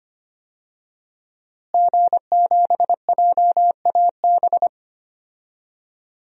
Morse code, heard as a keyed tone, about 25 wpm, 700 Hz. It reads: G7JAB
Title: DXCC-100 Call Signs - 5 Characters - Repeated 2x - 25wpm